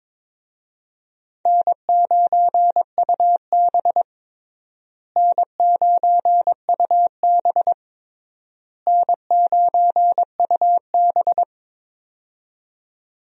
Morse code, heard as a keyed tone, about 21 wpm, 700 Hz. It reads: N9UB N9UB N9UB
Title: USA Call Signs - 4 Characters - Repeated 3x - 22wpm